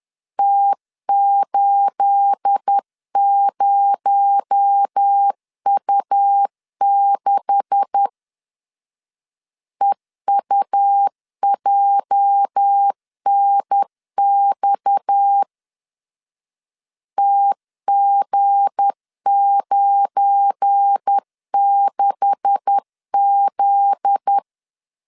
Morse code, heard as a keyed tone, about 10 wpm, 800 Hz. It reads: T80U6 EUJNX TG96Z
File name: CW / DET / 1 9 1500C S A 10wpm 4511